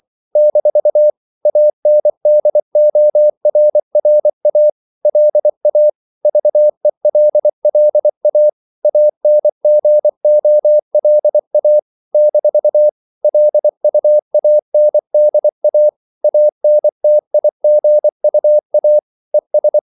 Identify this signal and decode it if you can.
Morse code, heard as a keyed tone, about 24 wpm, 600 Hz. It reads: - ANDORRA LA VELLA ANGOLA - LUANDA ANTIGUA ES
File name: CW / Practice / NationalCapitals24